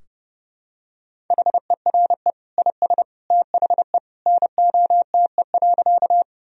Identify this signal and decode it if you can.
Morse code, heard as a keyed tone, about 28 wpm, 700 Hz. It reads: HERE IS THE NOTE.
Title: Sentences from Top 1000 Words - Review Part 2 - 30wpm